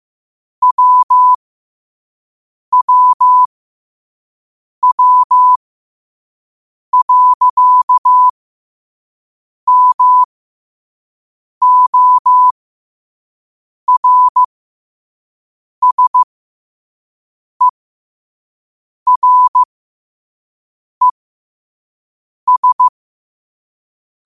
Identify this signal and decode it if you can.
Morse code, heard as a keyed tone, about 15 wpm, spaced as for 5 wpm, 1000 Hz. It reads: WWW.MORSERES
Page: New quotes every day in morse code at 5 Words per minute.